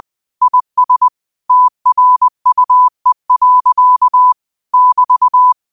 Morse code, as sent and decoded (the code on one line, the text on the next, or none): .. ... / - .-. ..- . .-.-.- / -...-
IS TRUE. <BT>